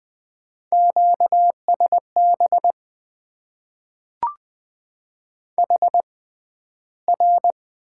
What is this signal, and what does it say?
Morse code, heard as a keyed tone, about 20 wpm, spaced as for 12 wpm, 700 Hz. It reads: QSB HR
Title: Top 90 QSO Elements - 6x Character Spacing - 20wpm